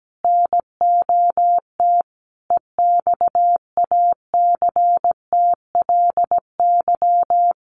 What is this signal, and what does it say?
Morse code, heard as a keyed tone, about 17 wpm, 700 Hz. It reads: NOT EXACTLY
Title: Sentences from Top 1000 Words 17wpm